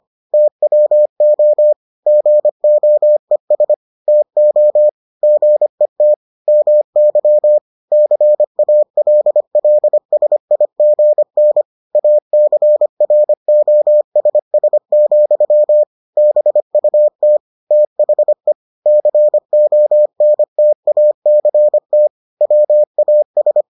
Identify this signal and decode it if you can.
Morse code, heard as a keyed tone, about 25 wpm, 600 Hz. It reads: TWO GOES TO GET MY CALLSIGN ACROSS, BUT THE CONTACT WAS